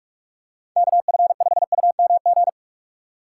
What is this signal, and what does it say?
Morse code, heard as a keyed tone, about 44 wpm, 700 Hz. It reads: KF5UMG
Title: USA Call Signs - 6 Characters - 3x Character Spacing - 45wpm